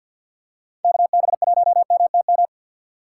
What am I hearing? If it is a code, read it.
Morse code, heard as a keyed tone, about 50 wpm, 700 Hz. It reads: KB1MTM
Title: USA Call Signs - 4x Character Spacing - 50wpm